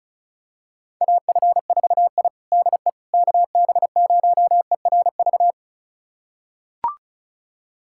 Morse code, heard as a keyed tone, about 34 wpm, 700 Hz. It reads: AF4I DE KB0ERV
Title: USA Callsign DE Callsign 35wpm